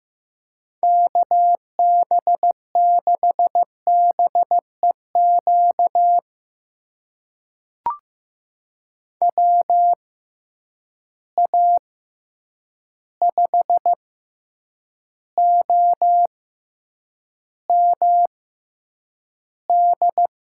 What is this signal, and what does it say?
Morse code, heard as a keyed tone, about 15 wpm, spaced as for 8 wpm, 700 Hz. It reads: KB6BEQ WA5OMD
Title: USA Call Signs - 6 Characters - 6x Character Spacing - 15wpm